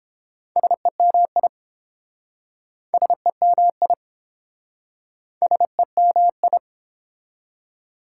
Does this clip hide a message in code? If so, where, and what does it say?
Yes: Morse code from 0.6 to 6.6 s, heard as a keyed tone, about 30 wpm, 700 Hz. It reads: SEMI SEMI SEMI